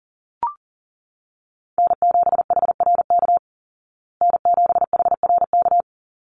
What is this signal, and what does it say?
Morse code, heard as a keyed tone, about 40 wpm, 700 Hz. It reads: N7HRK N7HRK
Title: Call Signs Repeated 2x - 40wpm